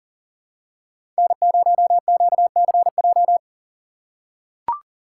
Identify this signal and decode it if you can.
Morse code, heard as a keyed tone, about 40 wpm, 700 Hz. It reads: N0QCJ
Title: USA Call Signs - 5 Characters - 2x Character Spacing - 40wpm